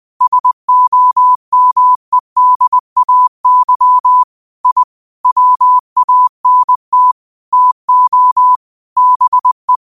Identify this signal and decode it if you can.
Morse code, heard as a keyed tone, about 20 wpm, 1000 Hz. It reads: SOMEDAY I WANT TO BE
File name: 20wpm